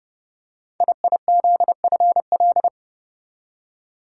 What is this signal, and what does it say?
Morse code, heard as a keyed tone, about 30 wpm, 700 Hz. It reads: IIZFL
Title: Code Groups - 5 Letters - 30wpm